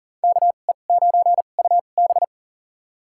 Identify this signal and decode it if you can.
Morse code, heard as a keyed tone, about 40 wpm, spaced as for 29 wpm, 700 Hz. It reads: KE9UB
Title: USA Call Signs - 5 Characters - 2x Character Spacing - 40wpm